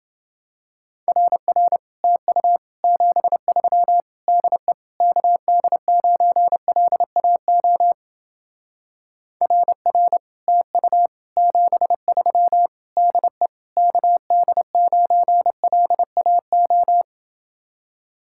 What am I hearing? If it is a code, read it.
Morse code, heard as a keyed tone, about 31 wpm, 700 Hz. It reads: RR TU 73 DE KD9LAO RR TU 73 DE KD9LAO